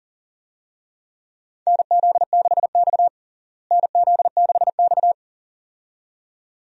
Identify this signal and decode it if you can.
Morse code, heard as a keyed tone, about 40 wpm, 700 Hz. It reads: NZ6X NZ6X
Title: USA Call Signs - 4 Characters - Repeated 2x - 40wpm